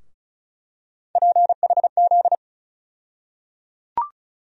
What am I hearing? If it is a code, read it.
Morse code, heard as a keyed tone, about 36 wpm, 700 Hz. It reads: PHZ
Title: USA Call Sign Postfixes - 3 Letters 35wpm